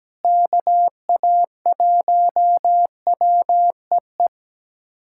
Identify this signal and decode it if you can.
Morse code, heard as a keyed tone, about 17 wpm, 700 Hz. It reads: KA1WEE
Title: USA Callsign DE Callsign - 6 Characters - 17wpm